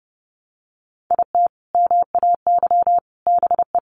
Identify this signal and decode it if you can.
Morse code, heard as a keyed tone, about 30 wpm, 700 Hz. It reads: IT MAY BE
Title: Sentences from Top 1500 Words 30wpm